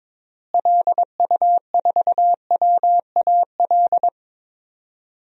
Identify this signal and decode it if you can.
Morse code, heard as a keyed tone, about 22 wpm, 700 Hz. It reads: LU4WAL
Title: DXCC-100 Call Signs - 6 Characters - Repeated 3x - 22wpm